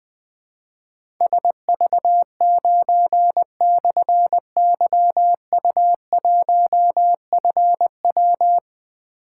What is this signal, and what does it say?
Morse code, heard as a keyed tone, about 20 wpm, 700 Hz. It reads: SV9/YU1FW
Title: Difficult Call Signs 20wpm